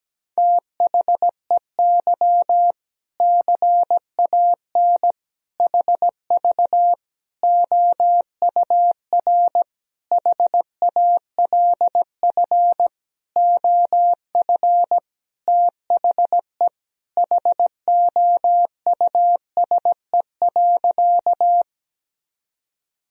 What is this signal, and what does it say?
Morse code, heard as a keyed tone, about 17 wpm, 700 Hz. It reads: THEY CAN HV OUR HALF OF THE HOUSE.
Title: Sentences from Top 400 Words and Abbreviations 17wpm